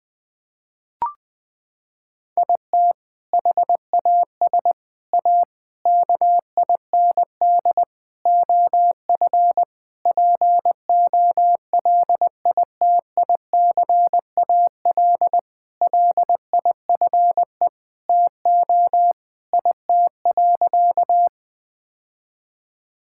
Morse code, heard as a keyed tone, about 20 wpm, 700 Hz. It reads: IT HAS A KIND OF POLITICAL LIFE TO IT.
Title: Sentences from Top 1000 Words - Review Part 5 - 20wpm